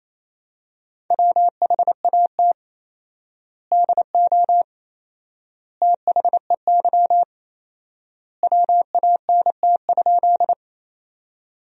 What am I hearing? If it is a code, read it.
Morse code, heard as a keyed tone, about 28 wpm, 700 Hz. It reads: WHAT DO THEY WANT?
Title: 4x Word-Spacing - Sentences from Top 200 Words - 28wpm